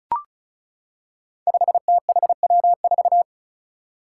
Morse code, heard as a keyed tone, about 34 wpm, 700 Hz. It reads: 5THW4
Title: Five Challenging Letter-Number Combinations 35wpm